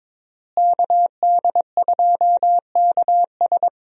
Morse code, heard as a keyed tone, about 22 wpm, 700 Hz. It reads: KD2KS
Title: USA Call Signs - 5 Characters - Repeated 3x - 22wpm